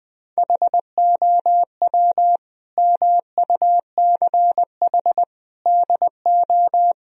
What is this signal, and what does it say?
Morse code, heard as a keyed tone, about 21 wpm, 700 Hz. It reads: HOW MUCH DO
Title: Sentences from Top 1000 Words - Review Part 2 - 20wpm